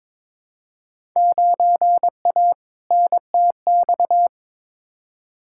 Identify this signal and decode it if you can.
Morse code, heard as a keyed tone, about 22 wpm, 700 Hz. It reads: 9A NTX
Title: Field Day Exchanges Encore - 22wpm